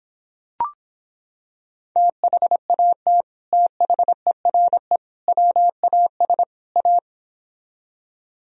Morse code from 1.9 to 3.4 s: THAT

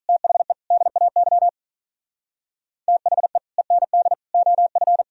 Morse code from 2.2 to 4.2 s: THE END

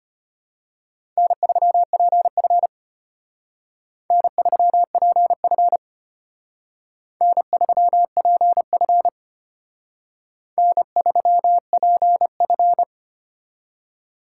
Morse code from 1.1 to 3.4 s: N3PF

